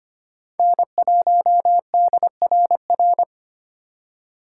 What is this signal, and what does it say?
Morse code, heard as a keyed tone, about 25 wpm, 700 Hz. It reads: N1DRR
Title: USA Call Signs - 3x Character Spacing - 25wpm